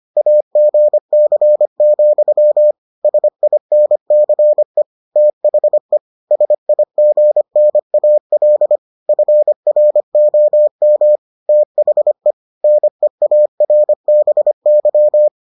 Morse code, heard as a keyed tone, about 25 wpm, 600 Hz. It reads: AGC, SINCE THE SIGNAL FROM THE NEARBY